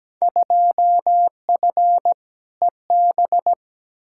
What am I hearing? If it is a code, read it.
Morse code, heard as a keyed tone, about 17 wpm, 700 Hz. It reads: 2F EB
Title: Field Day Exchanges Encore - 17wpm